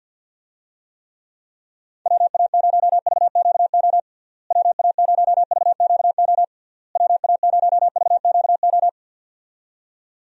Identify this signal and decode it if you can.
Morse code, heard as a keyed tone, about 50 wpm, 700 Hz. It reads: WA0VQO WA0VQO WA0VQO
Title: USA Call Signs - 6 Characters - Repeated 3x - 50wpm